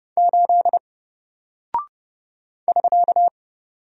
Morse code, heard as a keyed tone, about 30 wpm, 700 Hz. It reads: 8 <SK>
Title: Single Character (Letter:Number:Punctuation:Prosign) Alt 30wpm